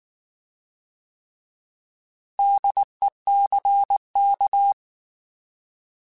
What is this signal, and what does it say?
Morse code, heard as a keyed tone, about 19 wpm, 800 Hz. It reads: DECK
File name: Alphabetical / Alphabetical mp3s / Ds @18